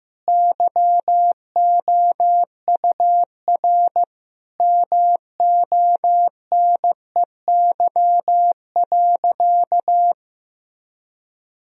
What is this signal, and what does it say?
Morse code, heard as a keyed tone, about 16 wpm, 700 Hz. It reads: YOUR MONEY.
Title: Sentences from Top 1000 Words - Review Part 2 - 15wpm